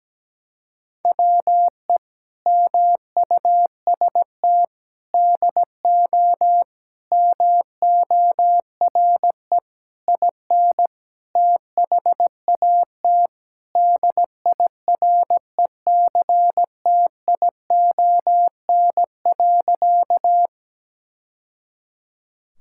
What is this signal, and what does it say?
Morse code, heard as a keyed tone, about 17 wpm, 700 Hz. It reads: WE MUST DO MORE IN THAT DIRECTION.
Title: Sentences from Top 1000 Words - Review Part 5 - 17wpm